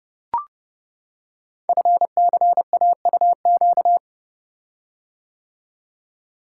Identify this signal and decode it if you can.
Morse code, heard as a keyed tone, about 29 wpm, 700 Hz. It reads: FCAUQ
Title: Code Groups - 5 Letters - 30wpm